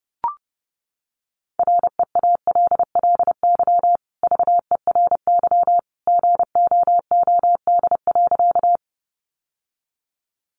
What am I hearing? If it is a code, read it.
Morse code, heard as a keyed tone, about 30 wpm, 700 Hz. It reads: REALLY VERY GOOD.
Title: Sentences from Top 600 Words 30wpm